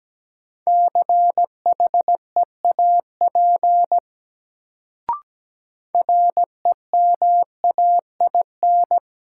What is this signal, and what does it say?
Morse code, heard as a keyed tone, about 17 wpm, 700 Hz. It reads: CHEAP REMAIN
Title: Sentences from Top 1200 Words - New Word Drill Alt - 17wpm